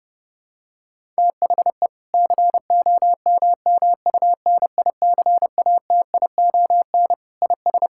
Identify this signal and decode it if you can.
Morse code, heard as a keyed tone, about 32 wpm, 700 Hz. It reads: THE COMMUNICATION IS
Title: Sentences from Top 1000 Words - Review Part 6 - 30wpm